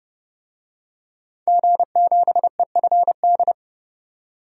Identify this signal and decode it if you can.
Morse code, heard as a keyed tone, about 32 wpm, 700 Hz. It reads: G7EFD
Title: DXCC-100 Call Signs - 5 Characters - 30wpm